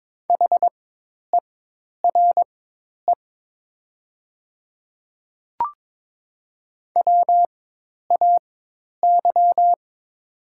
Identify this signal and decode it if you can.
Morse code, heard as a keyed tone, about 22 wpm, spaced as for 10 wpm, 700 Hz. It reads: HERE WAY
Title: 4x Character Spacing - Top 200 Words - 22wpm